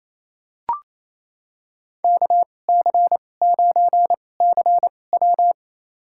Morse code, heard as a keyed tone, about 28 wpm, spaced as for 20 wpm, 700 Hz. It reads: KC9CW
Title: USA Call Signs - 2x Character Spacing - 28wpm